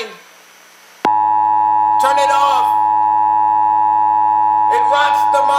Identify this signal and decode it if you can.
Emergency Alert System (SAME) tones (the 853 and 960 Hz attention tone), with speech. It is heard from 1.0 to 5.0 s.